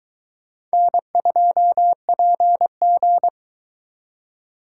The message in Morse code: N2PG

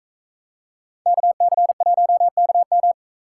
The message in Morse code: KC1KM

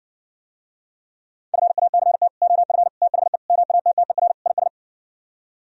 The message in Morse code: FACT OF THE MATTER IS

